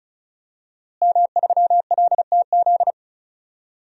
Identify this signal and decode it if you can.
Morse code, heard as a keyed tone, about 36 wpm, 700 Hz. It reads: M3LTZ
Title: DXCC-100 Call Signs - 5 Characters - Repeated 3x - 35wpm